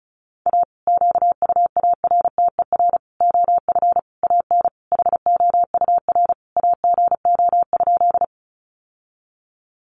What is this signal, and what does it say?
Morse code, heard as a keyed tone, about 34 wpm, 700 Hz. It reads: A QUARTER OF AN HOUR AGO?
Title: Sentences from Top 1300 Words 35wpm